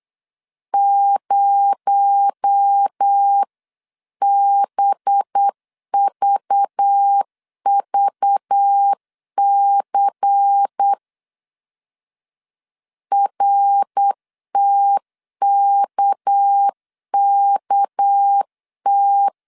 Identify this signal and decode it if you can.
Morse code, heard as a keyed tone, about 8 wpm, 800 Hz. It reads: 0BVVC RTKKT
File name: CW / DET / 1 6 1500C S B 8wpm 5142